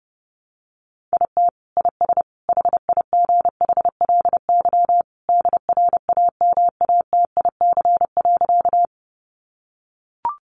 Morse code, heard as a keyed tone, about 30 wpm, 700 Hz. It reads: IT IS HIGHLY DRAMATIC.
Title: Sentences from Top 1300 Words 30wpm